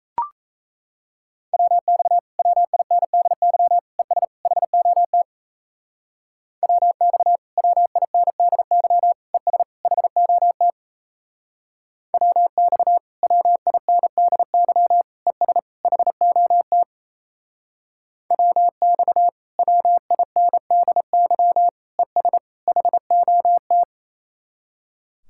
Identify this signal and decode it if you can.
Morse code, heard as a keyed tone, about 36 wpm, 700 Hz. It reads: WX WINDY ES HOT WX WINDY ES HOT WX WINDY ES HOT WX WINDY ES HOT